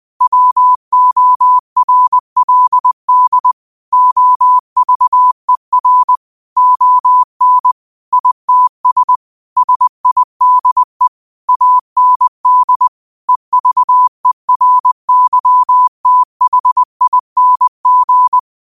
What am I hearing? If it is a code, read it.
Morse code, heard as a keyed tone, about 20 wpm, 1000 Hz. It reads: WORLD OVER ON ITS SIDE AND EVERYTHING
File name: qotdfiles / 20wpm